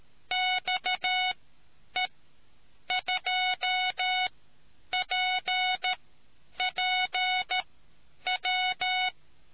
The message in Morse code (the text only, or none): XE2PPW